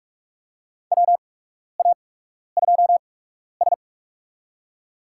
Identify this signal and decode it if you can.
Morse code, heard as a keyed tone, about 44 wpm, spaced as for 12 wpm, 700 Hz. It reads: WA2S
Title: USA Call Signs - 4 Characters - 8x Character Spacing - 45wpm